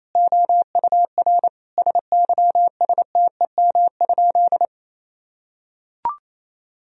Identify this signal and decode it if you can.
Morse code, heard as a keyed tone, about 27 wpm, 700 Hz. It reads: OUR SYSTEM?
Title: Sentences from Top 1000 Words - Review Part 3 - 28wpm